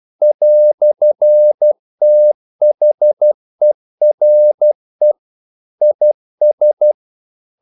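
Morse code, heard as a keyed tone, about 12 wpm, 600 Hz. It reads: "THERE IS
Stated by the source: Morse practice sending English quotations.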